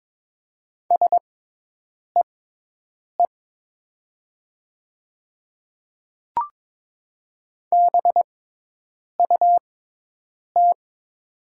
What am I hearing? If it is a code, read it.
Morse code, heard as a keyed tone, about 22 wpm, spaced as for 8 wpm, 700 Hz. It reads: SEE BUT
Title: 6x Character Spacing - Top 100 Words - 22wpm